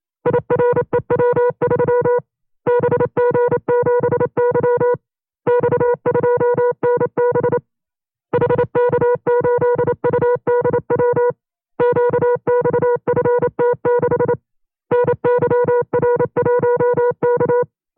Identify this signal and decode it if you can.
Morse code, heard as a keyed tone, about 27 wpm, 490 Hz. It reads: IREW3 BG7Y X2NB HK8UDW QXFT6 NYR1K
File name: FreeFiles / CW-practice / 25WPM-side1